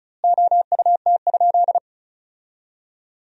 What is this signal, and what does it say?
Morse code, heard as a keyed tone, about 36 wpm, 700 Hz. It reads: OUT?
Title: 2x Word-Spacing - Sentences from Top 100 Words - 35wpm